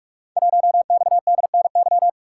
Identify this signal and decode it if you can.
Morse code, heard as a keyed tone, about 44 wpm, 700 Hz. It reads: 1XDNY
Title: Code Groups - 5 Letter:Numbers - 45wpm